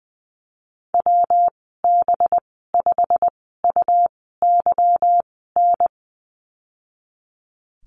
Morse code, heard as a keyed tone, about 20 wpm, spaced as for 14 wpm, 700 Hz. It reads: WB5UYN
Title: 219 Lesson 35g - Number 0 - Call Signs 20wpm